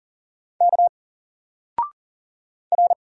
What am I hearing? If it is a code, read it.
Morse code, heard as a keyed tone, about 40 wpm, 700 Hz. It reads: K R
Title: Single Character (Letter:Number:Punctuation:Prosign) 40wpm